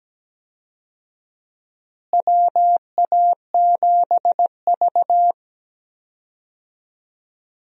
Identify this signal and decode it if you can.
Morse code, heard as a keyed tone, about 17 wpm, 700 Hz. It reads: WA7V
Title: USA Call Signs - 4 Characters - Repeated 2x - 17wpm